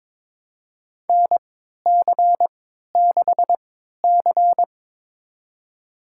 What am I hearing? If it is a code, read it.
Morse code, heard as a keyed tone, about 22 wpm, spaced as for 13 wpm, 700 Hz. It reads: NC6C
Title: USA Call Signs - 4 Characters - 3x Character Spacing - 22wpm